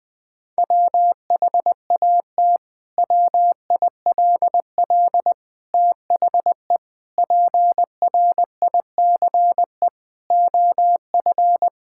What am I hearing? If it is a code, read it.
Morse code, heard as a keyed tone, about 20 wpm, 700 Hz. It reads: WHAT WILL THE PRICE OF
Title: Sentences from Top 1000 Words - Review Part 5 - 20wpm